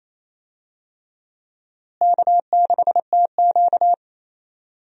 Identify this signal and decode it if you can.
Morse code, heard as a keyed tone, about 28 wpm, 700 Hz. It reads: K6TQ
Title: USA Call Signs - 4 Characters - Repeated 3x - 28wpm